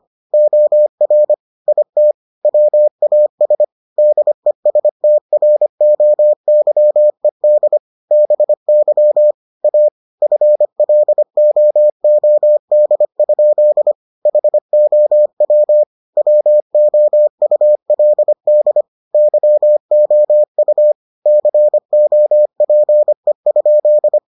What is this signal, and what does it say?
Morse code, heard as a keyed tone, about 25 wpm, 600 Hz. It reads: OR IT WAS DESTROYED BY A FLOOD? HOW WOULD YOU COPE?